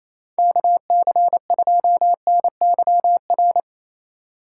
Morse code, heard as a keyed tone, about 27 wpm, 700 Hz. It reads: KC2NYR
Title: USA Call Signs - 2x Character Spacing - 28wpm